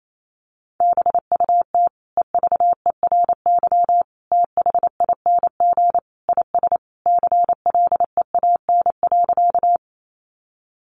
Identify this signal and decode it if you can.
Morse code, heard as a keyed tone, about 28 wpm, 700 Hz. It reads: BUT EVERYTHING IS CLEAN.
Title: Sentences from Top 1100 Words 28wpm